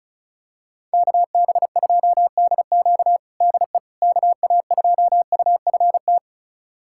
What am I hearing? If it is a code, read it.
Morse code, heard as a keyed tone, about 36 wpm, 690 Hz. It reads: KB2DQ DE KA2UFT
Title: USA Callsign DE Callsign 35wpm